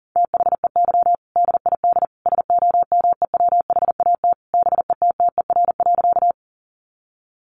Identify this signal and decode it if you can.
Morse code, heard as a keyed tone, about 40 wpm, 700 Hz. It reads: THEY DID SOMEWHAT BETTER.
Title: Sentences from Top 1300 Words 40wpm